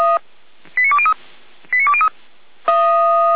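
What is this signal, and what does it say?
Five-tone selective call (ZVEI), heard as quick runs of five tones, tones of about 70 ms. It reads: 89203 89203
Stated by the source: ZVEI five-tone call 89230 received by SDR, from the feuerwehr-orlamuende.org fire brigade audio archive.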